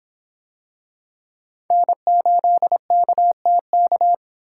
Morse code, heard as a keyed tone, about 23 wpm, 700 Hz. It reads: N8KTK